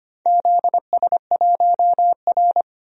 Morse code, heard as a keyed tone, about 25 wpm, 700 Hz. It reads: ZS1R
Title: DXCC-100 Call Signs - 4 Characters - 25wpm